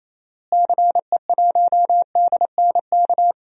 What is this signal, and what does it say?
Morse code, heard as a keyed tone, about 28 wpm, 700 Hz. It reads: CE1DNK
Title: DXCC-100 Call Signs - 6 Characters - Repeated 3x - 28wpm